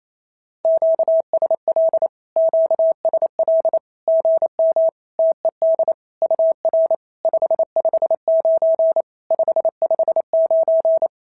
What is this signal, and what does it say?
Morse code, heard as a keyed tone, about 28 wpm, 650 Hz. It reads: QSL QSL GM TED UR 559 559
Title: SOTA QSO Practice 28wpm